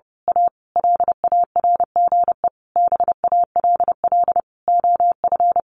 Morse code, heard as a keyed tone, about 30 wpm, 700 Hz. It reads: A LARGE BALL OF
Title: Easy Phrases 30wpm